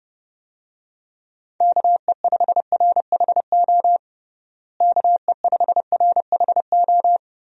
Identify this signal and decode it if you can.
Morse code, heard as a keyed tone, about 28 wpm, 700 Hz. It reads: KE5RHO KE5RHO
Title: USA Call Signs - 6 Characters - Repeated 3x - 30wpm